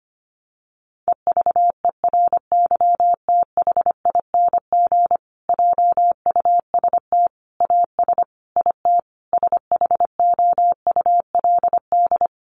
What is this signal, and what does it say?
Morse code, heard as a keyed tone, about 25 wpm, 700 Hz. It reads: EVERYTHING JUST AS IT SHOULD
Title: Sentences from Top 700 Words 25wpm